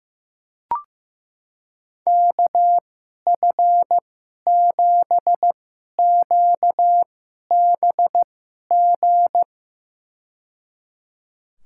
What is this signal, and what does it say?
Morse code, heard as a keyed tone, about 15 wpm, spaced as for 11 wpm, 700 Hz. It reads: KF7QBG
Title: USA Call Signs - 6 Characters - 2x Character Spacing - 15wpm